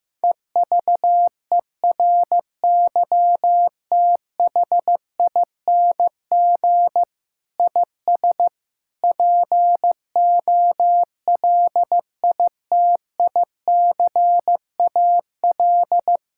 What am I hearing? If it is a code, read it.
Morse code, heard as a keyed tone, about 15 wpm, 700 Hz. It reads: EVERYTHING IS POLITICAL
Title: Sentences from Top 1000 Words - Review Part 5 - 15wpm